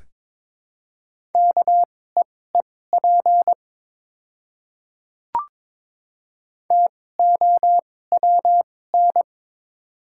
Morse code, heard as a keyed tone, about 22 wpm, spaced as for 16 wpm, 700 Hz. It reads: KEEP TOWN
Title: 2x Character Spacing - Top 500 Words - 22wpm